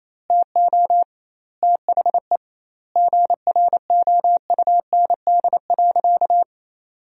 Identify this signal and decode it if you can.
Morse code, heard as a keyed tone, about 28 wpm, 700 Hz. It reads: TO THE GROUND.
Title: 2x Word-Spacing - Sentences from Top 500 Words - 28wpm